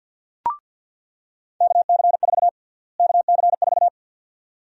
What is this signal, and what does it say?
Morse code, heard as a keyed tone, about 46 wpm, 700 Hz. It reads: KC4 KC4
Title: USA Call Sign Prefixes - Repeated 2x - 50wpm